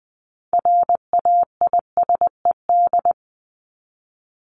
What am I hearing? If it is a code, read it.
Morse code, heard as a keyed tone, about 20 wpm, 700 Hz. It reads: RAISED
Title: ED Suffix - Top 100 Words with Base Words 20wpm